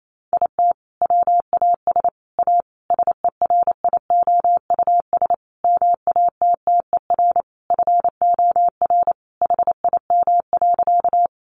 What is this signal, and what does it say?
Morse code, heard as a keyed tone, about 28 wpm, 700 Hz. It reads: IT WAS A SERIOUS MATTER FOR HIM.
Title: Sentences from Top 1100 Words 28wpm